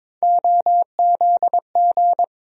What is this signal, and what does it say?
Morse code, heard as a keyed tone, about 22 wpm, 700 Hz. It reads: OZG
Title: USA Call Sign Postfixes - 3 Letters 22wpm